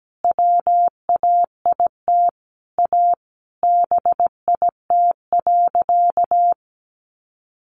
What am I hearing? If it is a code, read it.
Morse code, heard as a keyed tone, about 17 wpm, 700 Hz. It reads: WAIT A BIT.
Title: Sentences from Top 900 Words 17wpm